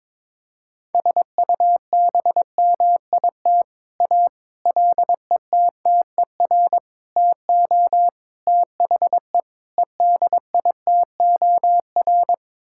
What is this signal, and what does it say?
Morse code, heard as a keyed tone, about 22 wpm, 700 Hz. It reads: SUBMIT A LETTER TO THE EDITOR
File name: practice / 22wpm / Sets of 6 Words - 22wpm